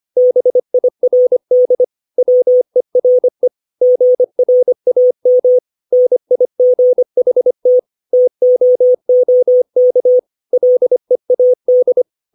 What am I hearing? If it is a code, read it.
Morse code, heard as a keyed tone, about 25 wpm, 500 Hz. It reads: BIRD WERE GRAM NIGHT TOOK LEAD
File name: 632 words / 632 Words 25WPM